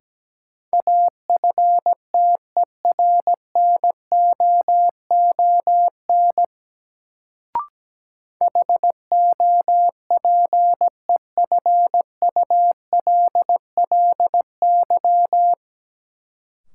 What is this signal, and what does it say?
Morse code, heard as a keyed tone, about 17 wpm, 700 Hz. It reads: AFTERNOON HOPEFULLY
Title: Sentences from Top 1500 Words - New Word Drill Alt - 17wpm